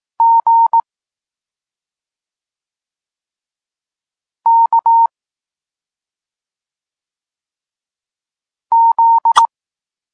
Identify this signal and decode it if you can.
Morse code, heard as a keyed tone, about 18 wpm, 920 Hz. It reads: G K Z
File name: Tekst-34 4